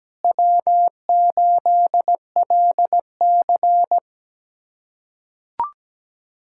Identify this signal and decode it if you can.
Morse code, heard as a keyed tone, about 17 wpm, 700 Hz. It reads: W8LC
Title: USA Callsign DE Callsign - 4 Characters - 17wpm